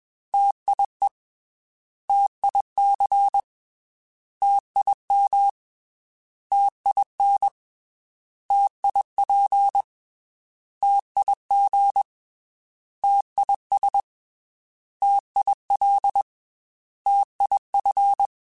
Morse code, heard as a keyed tone, about 21 wpm, 800 Hz. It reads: TIE TIC TIM TIN TIP TIG TIS TIL TIF
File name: T Prefixes / T mp3s